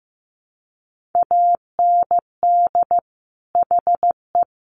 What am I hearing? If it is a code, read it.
Morse code, heard as a keyed tone, about 15 wpm, 700 Hz. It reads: AND HE
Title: Sentences from Top 1400 Words 15wpm